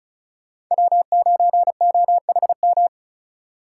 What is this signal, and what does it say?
Morse code, heard as a keyed tone, about 46 wpm, 700 Hz. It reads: W9OHM